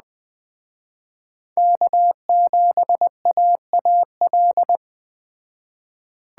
Morse code, heard as a keyed tone, about 21 wpm, 700 Hz. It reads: K7AAL
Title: Speed Racing - USA 1x3 Call Signs 17wpm